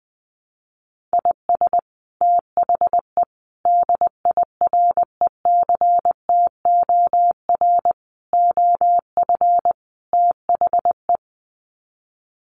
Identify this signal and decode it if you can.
Morse code, heard as a keyed tone, about 20 wpm, 700 Hz. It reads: IS THE DIRECTOR OF THE
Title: Sets of 5 Words Encore2 20wpm